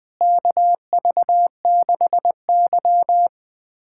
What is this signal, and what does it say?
Morse code, heard as a keyed tone, about 20 wpm, 700 Hz. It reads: KV6Y
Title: USA Callsign DE Callsign - 4 Characters - 20wpm